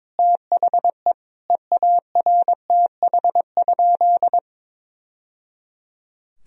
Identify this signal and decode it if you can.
Morse code, heard as a keyed tone, about 22 wpm, 700 Hz. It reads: THE EARTH?
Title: Sentences from Top 1000 Words - Review Part 2 - 22wpm